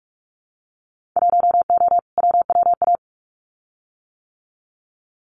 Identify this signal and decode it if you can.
Morse code, heard as a keyed tone, about 46 wpm, 700 Hz. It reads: 1O WWA
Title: Winter Field Day Exchanges 45wpm